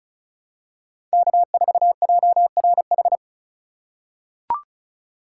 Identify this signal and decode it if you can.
Morse code, heard as a keyed tone, about 34 wpm, 700 Hz. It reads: K4JRH
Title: USA Call Signs - 4x Character Spacing - 35wpm